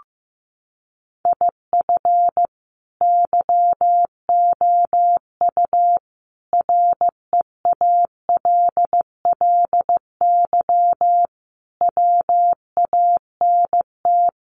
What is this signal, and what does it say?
Morse code, heard as a keyed tone, about 15 wpm, 700 Hz. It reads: IF YOU REALLY WANT
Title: Sets of 5 Words Encore2 15wpm